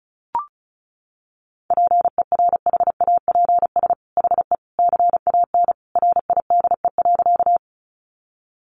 Morse code, heard as a keyed tone, about 34 wpm, 700 Hz. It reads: PERHAPS HE CAN RIDE.
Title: Sentences from Top 1200 Words 35wpm